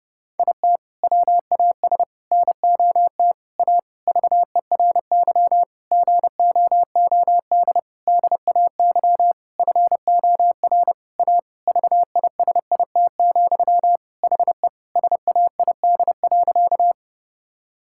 Morse code, heard as a keyed tone, about 32 wpm, 700 Hz. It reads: IT WAS NOT A VERY GOOD DAY FOR A VISIT, HE SAID.
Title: Sentences from Top 1000 Words - Review Part 6 - 30wpm